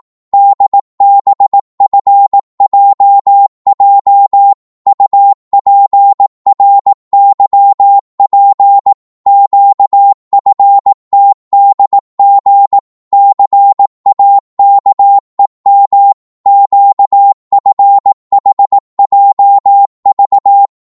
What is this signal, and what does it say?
Morse code, heard as a keyed tone, about 18 wpm, 800 Hz. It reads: DBFJJ UPRYP QFTDG CAKEM QFHJV
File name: Morse / ltr-18wpm